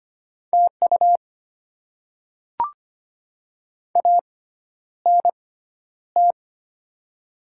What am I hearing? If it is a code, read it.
Morse code, heard as a keyed tone, about 25 wpm, spaced as for 11 wpm, 700 Hz. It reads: TU ANT
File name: Top 20 QSO Elements - 6x Character Spacing - 25wpm